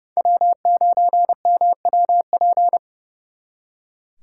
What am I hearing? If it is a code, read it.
Morse code, heard as a keyed tone, about 35 wpm, 700 Hz. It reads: W9MWP
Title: Speed Racing - USA 1x3 Call Signs 30wpm